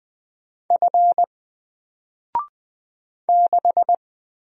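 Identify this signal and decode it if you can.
Morse code, heard as a keyed tone, about 20 wpm, 700 Hz. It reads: F 6
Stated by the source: Single letter and number Morse drill at 20 wpm, from N4LCD.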